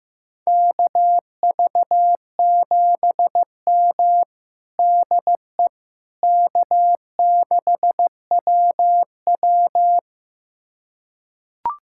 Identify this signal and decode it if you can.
Morse code, heard as a keyed tone, about 16 wpm, 700 Hz. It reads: KV7M DE K6WW
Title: USA Callsign DE Callsign - 4 Characters - 15wpm